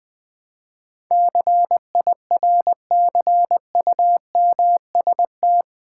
Morse code, heard as a keyed tone, about 20 wpm, 700 Hz. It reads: CIRCUMST
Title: Build A Word - 4 Letters At A Time - 12-Letter Words - 20wpm